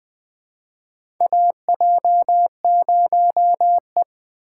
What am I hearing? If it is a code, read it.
Morse code, heard as a keyed tone, about 20 wpm, 700 Hz. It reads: AJ0E
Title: USA Call Signs - 3x Character Spacing - 20wpm